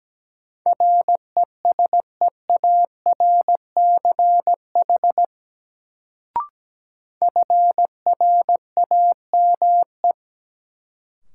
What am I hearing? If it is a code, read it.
Morse code, heard as a keyed tone, about 17 wpm, 700 Hz. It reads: RESEARCH FRAME
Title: Sentences from Top 1400 Words - New Word Drill Alt - 17wpm